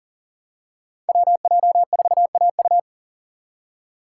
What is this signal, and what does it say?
Morse code, heard as a keyed tone, about 39 wpm, 700 Hz. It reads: WJ4AU
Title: Code Groups - 5 Letter:Numbers - 40wpm